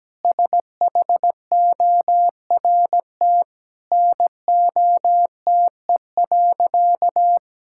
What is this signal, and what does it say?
Morse code, heard as a keyed tone, about 17 wpm, 700 Hz. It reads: SHORT NOTE.
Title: Sentences from Top 1000 Words - Review Part 6 - 17wpm